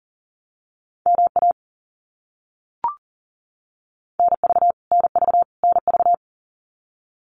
Morse code, heard as a keyed tone, about 40 wpm, 700 Hz. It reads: MA NV NV NV
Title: US States Repeated 3x 40wpm